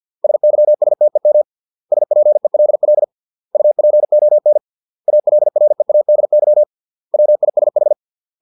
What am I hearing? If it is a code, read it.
Morse code, heard as a keyed tone, about 50 wpm, 600 Hz. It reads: SYSTEM SPELL UPON ALREADY WISH